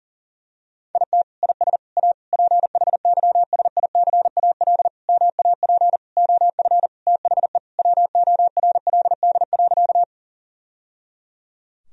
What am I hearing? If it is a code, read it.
Morse code, heard as a keyed tone, about 37 wpm, 700 Hz. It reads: IT IS A PHYSICAL MAP OF THE WORLD.
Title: Sentences from Top 1000 Words - Review Part 4 - 40wpm